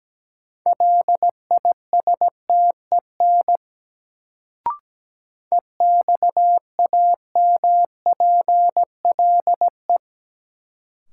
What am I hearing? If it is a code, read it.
Morse code, heard as a keyed tone, about 17 wpm, 700 Hz. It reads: LISTEN EXAMPLE
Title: Sentences from Top 400 Words - New Word Drill - 17wpm